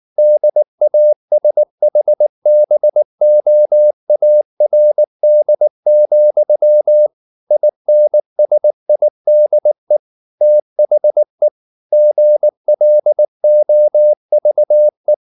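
Morse code, heard as a keyed tone, about 19 wpm, 600 Hz. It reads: DASHBOARD, INSIDE THE GLOVE
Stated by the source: Morse code at 19 wpm from CW med Gnister, reading Ken Follett's novel Never in English.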